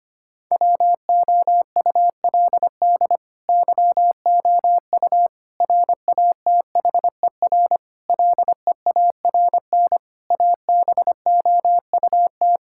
Morse code, heard as a keyed tone, about 25 wpm, 700 Hz. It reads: WOULD YOU RATHER LEARN ABOUT
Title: Sentences from Top 1000 Words - Review Part 6 - 25wpm